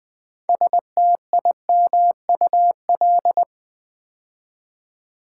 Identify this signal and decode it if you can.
Morse code, heard as a keyed tone, about 20 wpm, 700 Hz. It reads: STIMUL
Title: Build A Word - 2 Letters At A Time - 8-Letter Words - 20wpm